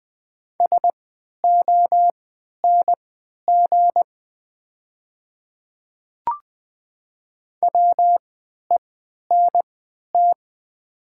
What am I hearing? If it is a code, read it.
Morse code, heard as a keyed tone, about 20 wpm, spaced as for 11 wpm, 700 Hz. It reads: SONG WENT
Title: 3x Character Spacing - Top 500 Words - 20wpm